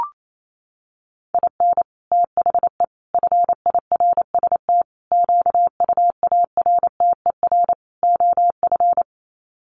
Morse code, heard as a keyed tone, about 28 wpm, 700 Hz. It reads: IN THE FIRST QUARTER OF
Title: Sets of 5 Words Encore2 28wpm